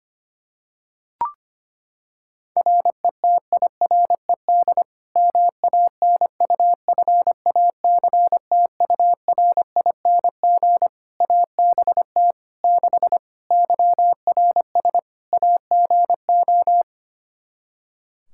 Morse code, heard as a keyed tone, about 25 wpm, 700 Hz. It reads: RETIRED MANUFACTURING ABT 6 YRS AGO
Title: Practice QSO - Exchange 3 - More Info - 25wpm